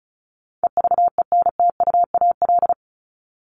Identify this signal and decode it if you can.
Morse code, heard as a keyed tone, about 34 wpm, 700 Hz. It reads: EVENTUAL
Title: LY Suffix - Top 100 Words with Base Words 35wpm